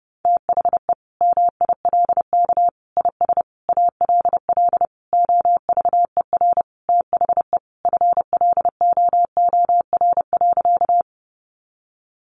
Morse code, heard as a keyed tone, about 30 wpm, 700 Hz. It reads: THE MILK IS ALL OVER THE FLOOR.
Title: Additional Easy Phrases 30wpm